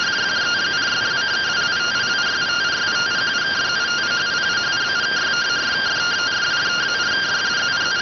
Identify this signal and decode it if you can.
Two-tone teleprinter signal, about 170 Hz shift.